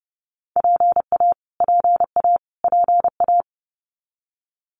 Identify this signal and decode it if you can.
Morse code, heard as a keyed tone, about 30 wpm, 700 Hz. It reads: PA PA PA
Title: US States Repeated 3x 30wpm